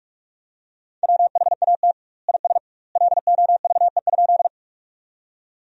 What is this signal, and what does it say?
Morse code, heard as a keyed tone, about 46 wpm, 700 Hz. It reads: WHAT IS LOVE?
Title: 2x Word-Spacing - Sentences from Top 400 Words - 45wpm